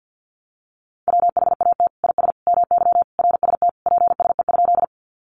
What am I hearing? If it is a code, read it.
Morse code, heard as a keyed tone, about 50 wpm, 700 Hz. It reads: WHAT IS MY RST PSE?
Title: Common QSO Phrases 50wpm